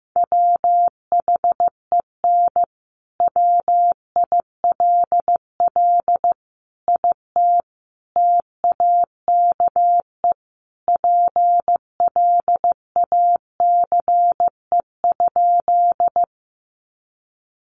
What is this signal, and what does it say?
Morse code, heard as a keyed tone, about 15 wpm, 700 Hz. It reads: WHEN WILL IT TAKE PLACE?
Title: Sentences from Top 200 Words 15wpm